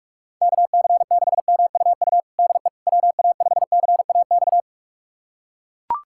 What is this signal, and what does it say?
Morse code, heard as a keyed tone, about 41 wpm, 700 Hz. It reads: KC6MUU DE WA5CAX
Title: USA Callsign DE Callsign - 6 Characters - 45wpm